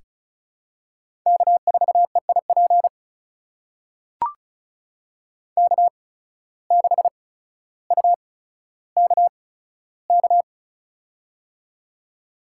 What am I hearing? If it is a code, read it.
Morse code, heard as a keyed tone, about 36 wpm, spaced as for 15 wpm, 700 Hz. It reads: K4EIP K6UKK
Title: USA Call Signs - 5 Characters - 8x Character Spacing - 35wpm